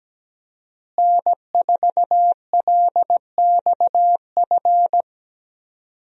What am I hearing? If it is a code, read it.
Morse code, heard as a keyed tone, about 17 wpm, 700 Hz. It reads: N4LXF
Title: USA Call Signs - 5 Characters - Repeated 2x - 17wpm